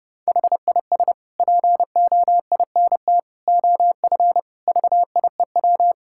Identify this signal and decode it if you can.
Morse code, heard as a keyed tone, about 29 wpm, 700 Hz. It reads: HIS POINT OF VIEW
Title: Sentences from Top 1000 Words - Review Part 6 - 30wpm